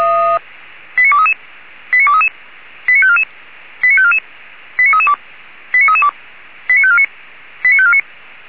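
Five-tone selective call (ZVEI), heard as quick runs of five tones, tones of about 70 ms. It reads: 89233 89233 89544 89544 89332 89332 89549 89549
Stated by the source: Five-tone ZVEI selective call received on an SDR.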